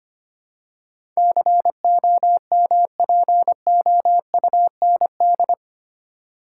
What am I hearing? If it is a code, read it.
Morse code, heard as a keyed tone, about 25 wpm, 700 Hz. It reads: COMPOUND
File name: Eight Letter Words - 25wpm